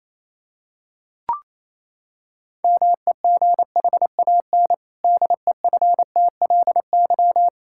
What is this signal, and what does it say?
Morse code, heard as a keyed tone, about 28 wpm, 700 Hz. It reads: MEGHAN DEFTLY
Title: Pangram Challenge v2 - 28wpm